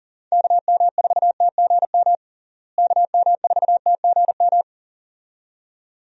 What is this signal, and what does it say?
Morse code, heard as a keyed tone, about 40 wpm, 700 Hz. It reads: KM4TGM KM4TGM
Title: USA Call Signs - 6 Characters - Repeated 2x - 40wpm